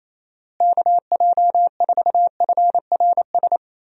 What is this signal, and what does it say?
Morse code, heard as a keyed tone, about 28 wpm, 700 Hz. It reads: KJ4FRS
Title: USA Call Signs - 6 Characters - Repeated 3x - 28wpm